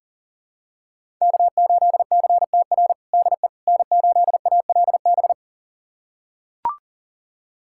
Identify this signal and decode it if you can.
Morse code, heard as a keyed tone, about 40 wpm, 700 Hz. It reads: K8CTR DE N8ALB
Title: USA Callsign DE Callsign - 5 Characters - 40wpm